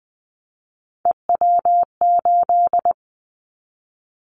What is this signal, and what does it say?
Morse code, heard as a keyed tone, about 20 wpm, 700 Hz. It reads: EW8
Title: Callsign Prefixes for Top 50 DXCC - 20wpm